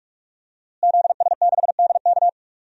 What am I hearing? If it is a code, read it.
Morse code, heard as a keyed tone, about 41 wpm, 700 Hz. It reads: ZS6DK
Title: DXCC-100 Call Signs - 5 Characters - Repeated 2x - 45wpm